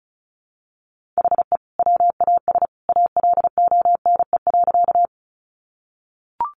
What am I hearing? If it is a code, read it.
Morse code, heard as a keyed tone, about 34 wpm, 700 Hz. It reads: HE WAS ALONE.
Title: Sentences from Top 1000 Words 35wpm